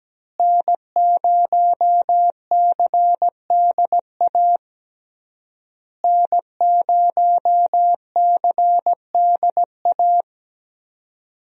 Morse code, heard as a keyed tone, about 17 wpm, 700 Hz. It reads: N0CDA N0CDA
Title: USA Call Signs - 5 Characters - Repeated 3x - 17wpm